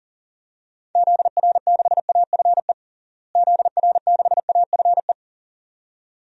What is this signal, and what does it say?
Morse code, heard as a keyed tone, about 40 wpm, 700 Hz. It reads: ZR6AFE ZR6AFE
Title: DXCC-100 Call Signs - 6 Characters - Repeated 2x - 40wpm